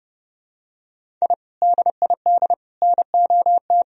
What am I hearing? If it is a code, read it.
Morse code, heard as a keyed tone, about 30 wpm, 700 Hz. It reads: I DID NOT
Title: Sentences from Top 1000 Words - Review Part 1 - 30wpm